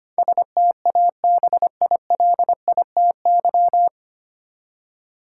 Morse code, heard as a keyed tone, about 25 wpm, 700 Hz. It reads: STABILITY
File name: Nine Letter Words - 25wpm